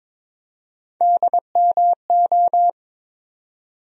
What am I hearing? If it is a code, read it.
Morse code, heard as a keyed tone, about 22 wpm, 700 Hz. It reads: DMO
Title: USA Call Sign Postfixes - 3 Letters 22wpm